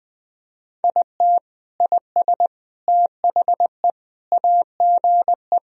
Morse code, heard as a keyed tone, about 20 wpm, 700 Hz. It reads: IT IS THE AGE
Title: Sentences from Top 1000 Words - Review Part 3 - 20wpm